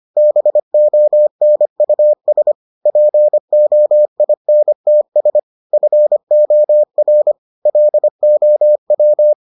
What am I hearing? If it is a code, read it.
Morse code, heard as a keyed tone, about 25 wpm, 600 Hz. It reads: BONUS POINTS FOR LOW